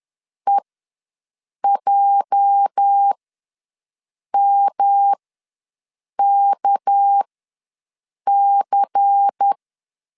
Morse code, heard as a keyed tone, about 10 wpm, spaced as for 6 wpm, 800 Hz. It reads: EJMKC
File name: LEZIONI / 10 SIGNS / 10C DOT ws 10wpm 4213